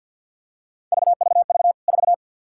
Morse code, heard as a keyed tone, about 53 wpm, 700 Hz. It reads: VVV 4
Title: K1USN SST QSO Practice 50wpm